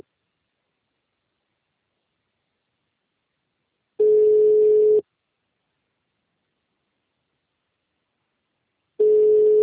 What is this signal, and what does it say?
Ringback tone (Europe, 425 Hz).